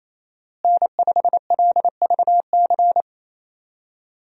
Morse code, heard as a keyed tone, about 28 wpm, 700 Hz. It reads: N5LVC
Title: USA Call Signs - 5 Characters - Repeated 2x - 28wpm